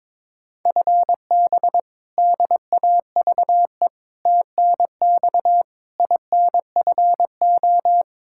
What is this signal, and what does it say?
Morse code, heard as a keyed tone, about 22 wpm, 700 Hz. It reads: FB DAVE TNX INFO
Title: K1USN SST QSO Practice 22wpm